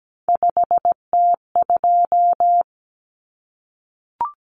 Morse code, heard as a keyed tone, about 17 wpm, 700 Hz. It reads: 5T2
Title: Common Cut Numbers - 3-Digit 17wpm